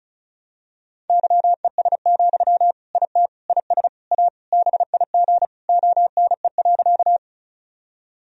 Morse code, heard as a keyed tone, about 36 wpm, 700 Hz. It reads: YES, IT IS A BIG ONE.
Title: Sentences from Top 1000 Words - Review Part 1 - 35wpm